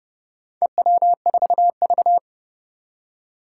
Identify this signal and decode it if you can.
Morse code, heard as a keyed tone, about 30 wpm, 700 Hz. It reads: EW4V
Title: Four Challenging Letter-Number Combinations 30wpm